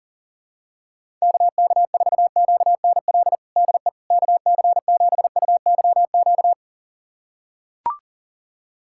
Morse code, heard as a keyed tone, about 40 wpm, 700 Hz. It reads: KK4QNL DE KC7UYQ
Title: USA Callsign DE Callsign - 6 Characters - 40wpm